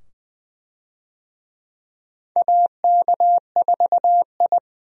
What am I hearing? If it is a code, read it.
Morse code, heard as a keyed tone, about 20 wpm, 700 Hz. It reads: AK4I
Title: USA Call Signs - 4 Characters - 20wpm